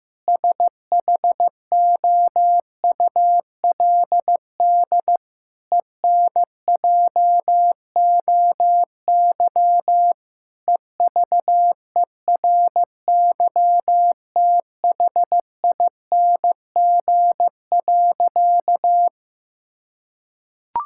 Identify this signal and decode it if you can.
Morse code, heard as a keyed tone, about 15 wpm, 700 Hz. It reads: SHOULD ENJOY EVERYTHING.
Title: Sentences from Top 1000 Words - Review Part 5 - 15wpm